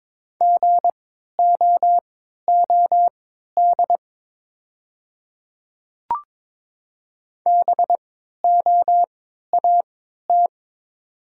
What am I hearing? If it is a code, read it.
Morse code, heard as a keyed tone, about 23 wpm, spaced as for 13 wpm, 700 Hz. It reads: GOOD BOAT
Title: 3x Character Spacing - Top 500 Words - 22wpm